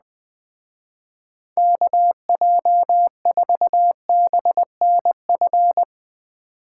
Morse code, heard as a keyed tone, about 20 wpm, 700 Hz. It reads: KJ4BNF
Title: Speed Racing - USA 2x3 Call Signs 17wpm